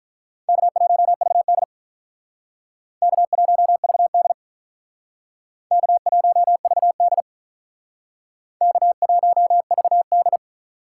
Morse code, heard as a keyed tone, about 45 wpm, 700 Hz. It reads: K1VD K1VD K1VD K1VD